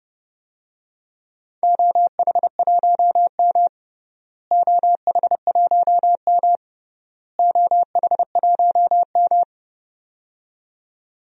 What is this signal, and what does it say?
Morse code, heard as a keyed tone, about 30 wpm, 700 Hz. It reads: OH1M OH1M OH1M
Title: DXCC-100 Call Signs - 4 Characters - Repeated 3x - 30wpm